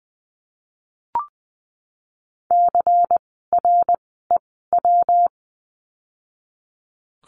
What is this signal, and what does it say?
Morse code, heard as a keyed tone, about 20 wpm, spaced as for 15 wpm, 700 Hz. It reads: CREW
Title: 115 Lesson 19f - Letter w - Words 20wpm